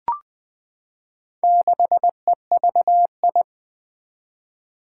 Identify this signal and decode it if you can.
Morse code, heard as a keyed tone, about 20 wpm, 700 Hz. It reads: 6EVI
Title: Four Challenging Letter-Number Combinations 20wpm